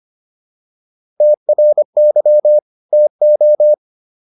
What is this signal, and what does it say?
Morse code, heard as a keyed tone, about 25 wpm, 600 Hz. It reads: TRY TO
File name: WD104 25